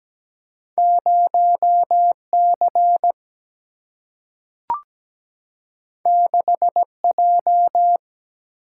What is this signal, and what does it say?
Morse code, heard as a keyed tone, about 17 wpm, 700 Hz. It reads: 0 C 6 J
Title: Single-Letter Single-Number 17wpm